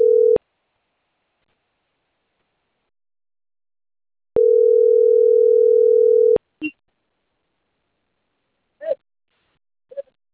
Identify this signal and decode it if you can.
Ringback tone (North America), 440 + 480 Hz.